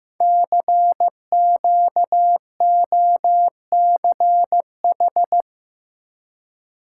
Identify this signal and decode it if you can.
Morse code, heard as a keyed tone, about 15 wpm, 700 Hz. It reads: CQOCH
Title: Code Groups - 5 Letter:Numbers - 15wpm